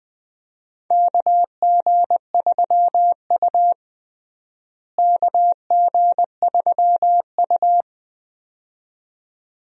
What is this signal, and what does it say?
Morse code, heard as a keyed tone, about 20 wpm, 700 Hz. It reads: KG3U KG3U
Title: USA Call Signs - 4 Characters - Repeated 2x - 20wpm